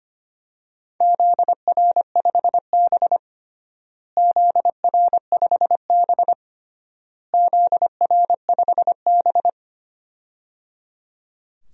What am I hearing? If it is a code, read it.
Morse code, heard as a keyed tone, about 25 wpm, 700 Hz. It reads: ZR5B ZR5B ZR5B
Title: DXCC-100 Call Signs - 4 Characters - Repeated 3x - 25wpm